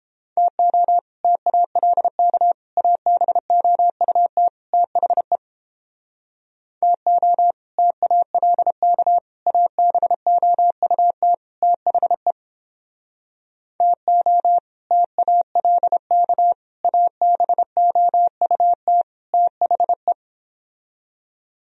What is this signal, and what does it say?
Morse code, heard as a keyed tone, about 29 wpm, 700 Hz. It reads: TO TALK ABOUT THE TO TALK ABOUT THE TO TALK ABOUT THE